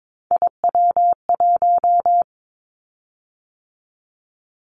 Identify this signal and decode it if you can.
Morse code, heard as a keyed tone, about 22 wpm, 700 Hz. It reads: IW1
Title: Callsign Prefixes for Top 25 DXCC - 22wpm